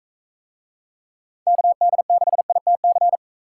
Morse code, heard as a keyed tone, about 44 wpm, 700 Hz. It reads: KD6ITC